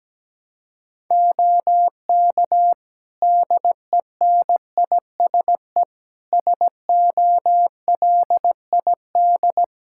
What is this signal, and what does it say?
Morse code, heard as a keyed tone, about 17 wpm, 700 Hz. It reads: OK DENISE SOLID
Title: Practice QSO - Exchange 3 - More Info - 17wpm